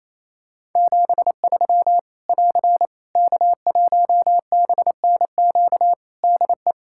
Morse code, heard as a keyed tone, about 28 wpm, 700 Hz. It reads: 73 <AR> K1BNQ DE
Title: Practice QSO - Exchange 4 - Ending - 28wpm